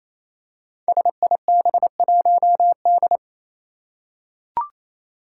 Morse code, heard as a keyed tone, about 28 wpm, 700 Hz. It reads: SIB1D
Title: Five Challenging Letter-Number Combinations 28wpm